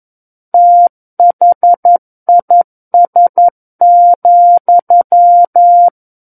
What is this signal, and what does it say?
Morse code, heard as a keyed tone, about 11 wpm, 700 Hz. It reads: THIS,
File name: 11 WPM